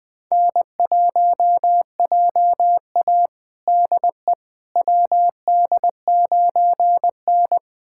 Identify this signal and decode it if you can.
Morse code, heard as a keyed tone, about 21 wpm, 700 Hz. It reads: N1JA DE WD9N
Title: USA Callsign DE Callsign - 4 Characters - 20wpm